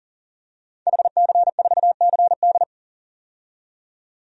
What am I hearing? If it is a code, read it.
Morse code, heard as a keyed tone, about 38 wpm, 700 Hz. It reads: HC4CD
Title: DXCC-100 Call Signs - 5 Characters - 40wpm